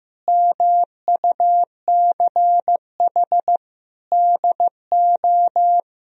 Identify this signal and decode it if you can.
Morse code, heard as a keyed tone, about 16 wpm, 700 Hz. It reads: MUCH DO
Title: Sentences from Top 1000 Words - Review Part 3 - 15wpm